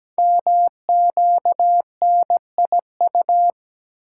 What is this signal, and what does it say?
Morse code, heard as a keyed tone, about 17 wpm, 700 Hz. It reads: MQNIU
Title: Code Groups - 5 Letters - 17wpm